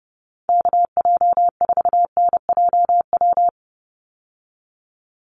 Morse code, heard as a keyed tone, about 29 wpm, 700 Hz. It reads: KJ4NJW
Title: USA Call Signs 30wpm